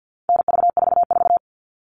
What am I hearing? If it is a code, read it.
Morse code, heard as a keyed tone, about 50 wpm, 700 Hz. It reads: NV44
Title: All Cut Numbers - 4-Digit 50wpm